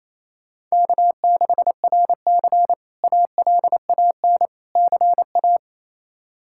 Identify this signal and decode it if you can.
Morse code, heard as a keyed tone, about 27 wpm, 700 Hz. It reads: K6RC ALAN CA
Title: K1USN SST QSO Practice - 28wpm